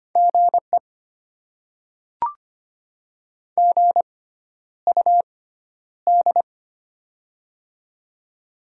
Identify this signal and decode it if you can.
Morse code, heard as a keyed tone, about 25 wpm, spaced as for 11 wpm, 700 Hz. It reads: GE GUD
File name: Top 20 QSO Elements - 6x Character Spacing - 25wpm